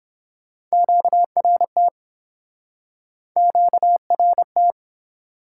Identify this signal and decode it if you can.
Morse code, heard as a keyed tone, about 31 wpm, 700 Hz. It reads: QRT QRT